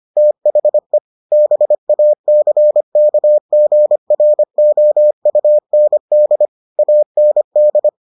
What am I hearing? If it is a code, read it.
Morse code, heard as a keyed tone, about 25 wpm, 600 Hz. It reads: THE BACKGROUND AND